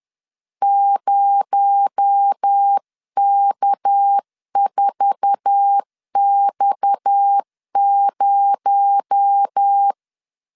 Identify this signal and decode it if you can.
Morse code, heard as a keyed tone, about 10 wpm, 800 Hz. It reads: 0K4X0
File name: CW / DET / 1 9 1500C S A 10wpm 4511